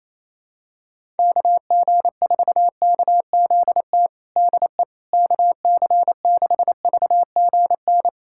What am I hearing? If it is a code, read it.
Morse code, heard as a keyed tone, about 28 wpm, 700 Hz. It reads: KG4KZT DE KC6VGN
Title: USA Callsign DE Callsign 28wpm